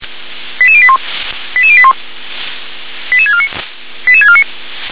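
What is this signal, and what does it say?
Five-tone selective call (ZVEI), heard as quick runs of five tones, tones of about 70 ms. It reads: 88081 88081 88649 88649
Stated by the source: ZVEI five-tone selective call received on an SDR.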